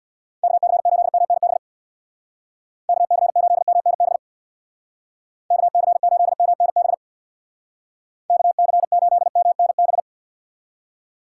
Morse code, heard as a keyed tone, about 61 wpm, 700 Hz. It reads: KC8MNB KC8MNB KC8MNB KC8MNB
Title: Speed Racing - USA 2x3 Call Signs 50wpm